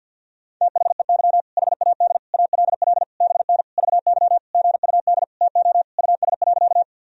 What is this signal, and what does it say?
Morse code, heard as a keyed tone, about 52 wpm, 700 Hz. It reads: THEY HAD ALL BN VY GUD TO US.
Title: Sentences from Top 200 Words and Abbreviations 50wpm